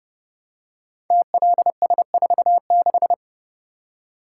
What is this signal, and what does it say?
Morse code, heard as a keyed tone, about 30 wpm, 700 Hz. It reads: TLS46
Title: Code Groups - 5 Letter:Numbers - 30wpm